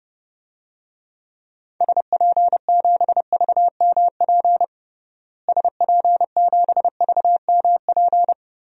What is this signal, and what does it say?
Morse code, heard as a keyed tone, about 30 wpm, 700 Hz. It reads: SP7VMP SP7VMP
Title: DXCC-100 Call Signs - 6 Characters - Repeated 3x - 30wpm